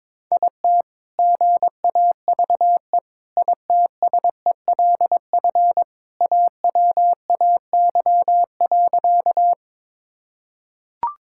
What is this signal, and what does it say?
Morse code, heard as a keyed tone, about 22 wpm, 700 Hz. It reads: IT GAVE ITSELF AWAY.
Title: Sentences from Top 1000 Words - Review Part 5 - 22wpm